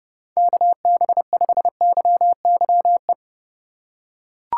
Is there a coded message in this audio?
Yes: Morse code from 0.4 to 3.1 s, heard as a keyed tone, about 29 wpm, 700 Hz. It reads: KB5YYE